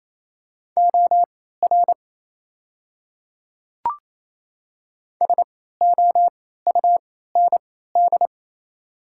Morse code, heard as a keyed tone, about 28 wpm, spaced as for 16 wpm, 690 Hz. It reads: OR SOUND
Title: 3x Character Spacing - Top 200 Words - 28wpm